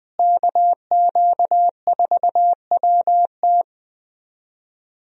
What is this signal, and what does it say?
Morse code, heard as a keyed tone, about 20 wpm, 700 Hz. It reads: KQ4WT
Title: USA Call Signs - 5 Characters - Repeated 3x - 20wpm